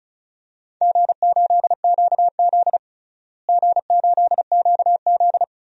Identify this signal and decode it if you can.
Morse code, heard as a keyed tone, about 36 wpm, 700 Hz. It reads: G8QZ G8QZ
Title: DXCC-100 Call Signs - 4 Characters - Repeated 2x - 35wpm